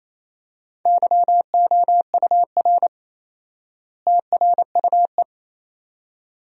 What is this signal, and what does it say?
Morse code, heard as a keyed tone, about 27 wpm, 700 Hz. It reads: YOUR TRUE
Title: 4x Word-Spacing - Sentences from Top 400 Words - 28wpm